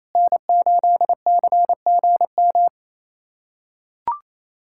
Morse code, heard as a keyed tone, about 28 wpm, 700 Hz. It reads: N8CGM
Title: USA Callsign DE Callsign - 5 Characters - 28wpm